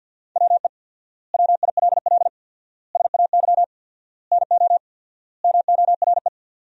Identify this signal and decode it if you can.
Morse code, heard as a keyed tone, about 52 wpm, 700 Hz. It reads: WE WILL SAY NO MORE
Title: 4x Word-Spacing - Sentences from Top 200 Words - 50wpm